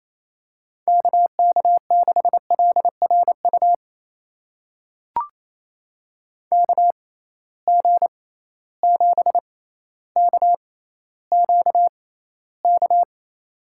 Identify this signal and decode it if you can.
Morse code, heard as a keyed tone, about 28 wpm, spaced as for 14 wpm, 700 Hz. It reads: KK6LRU KG7KQK
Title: USA Call Signs - 6 Characters - 6x Character Spacing - 28wpm